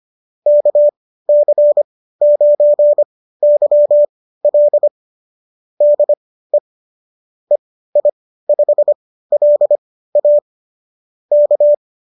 Morse code, heard as a keyed tone, about 25 wpm, spaced as for 15 wpm, 590 Hz. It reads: KC9YL DE EI5LA K